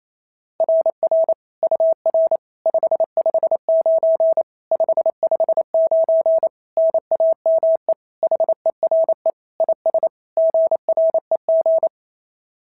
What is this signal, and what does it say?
Morse code, heard as a keyed tone, about 27 wpm, 650 Hz. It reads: RR UR 559 559 NAME HERE IS GREG
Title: SOTA QSO Practice 28wpm